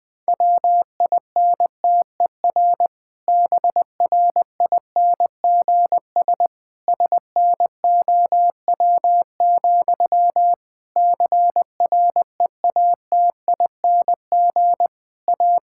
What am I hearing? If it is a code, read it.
Morse code, heard as a keyed tone, about 20 wpm, 700 Hz. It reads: WINTER BRINGS SNOW, CREATING A
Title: Difficult Sentences with Hints - 20wpm